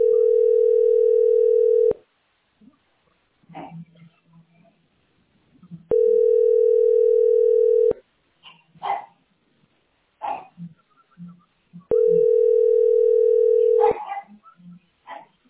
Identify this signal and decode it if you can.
Ringback tone (North America), 440 + 480 Hz.
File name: 43da-48ab-b100